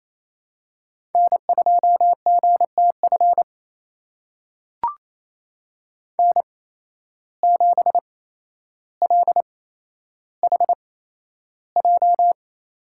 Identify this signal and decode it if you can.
Morse code, heard as a keyed tone, about 28 wpm, spaced as for 12 wpm, 700 Hz. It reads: N2GTF N7LHJ
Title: USA Call Signs - 5 Characters - 8x Character Spacing - 28wpm